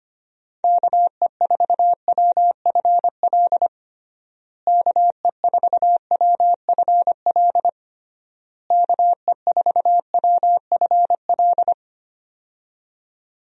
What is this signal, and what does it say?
Morse code, heard as a keyed tone, about 25 wpm, 700 Hz. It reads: KE4WFL KE4WFL KE4WFL
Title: USA Call Signs - 6 Characters - Repeated 3x - 25wpm